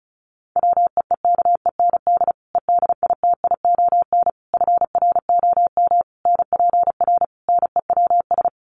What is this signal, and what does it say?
Morse code, heard as a keyed tone, about 34 wpm, 700 Hz. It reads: WEEKEND EDITION FROM NPR NEWS
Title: Sets of 5 Words 35wpm